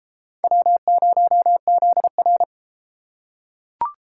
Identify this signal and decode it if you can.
Morse code, heard as a keyed tone, about 28 wpm, 700 Hz. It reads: W0ZR